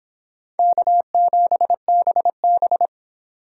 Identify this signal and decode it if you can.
Morse code, heard as a keyed tone, about 29 wpm, 700 Hz. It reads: K7BB